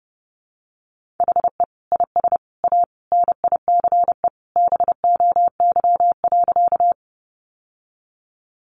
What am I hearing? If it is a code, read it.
Morse code, heard as a keyed tone, about 30 wpm, 700 Hz. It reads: HE IS A NICE BOY.
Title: Sentences from Top 1100 Words 30wpm